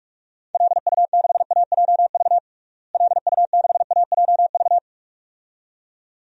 Morse code, heard as a keyed tone, about 42 wpm, 700 Hz. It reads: LU6AJV LU6AJV
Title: DXCC-100 Call Signs - 6 Characters - Repeated 2x - 45wpm